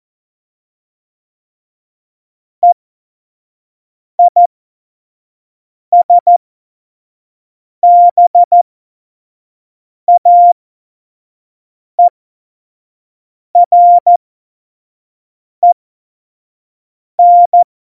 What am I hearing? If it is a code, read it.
Morse code, heard as a keyed tone, about 14 wpm, spaced as for 5 wpm, 700 Hz. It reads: EISBAEREN
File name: Morsekurs / Witze / Witz 29. August 17, Tempo 25 F70